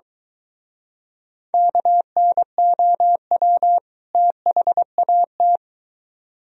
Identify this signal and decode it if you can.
Morse code, heard as a keyed tone, about 23 wpm, 700 Hz. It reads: KNOW THAT